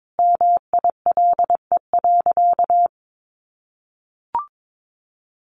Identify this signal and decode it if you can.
Morse code, heard as a keyed tone, about 22 wpm, 700 Hz. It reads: MILE.
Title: Sentences from Top 1300 Words 22wpm